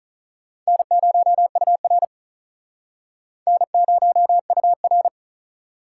Morse code, heard as a keyed tone, about 39 wpm, 700 Hz. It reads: N0UR N0UR
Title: Speed Racing - USA Call Signs 35wpm